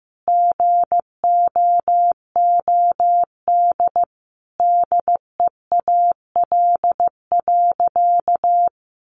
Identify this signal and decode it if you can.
Morse code, heard as a keyed tone, about 15 wpm, 700 Hz. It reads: GOOD DEAL.
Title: Sentences from Top 1200 Words 15wpm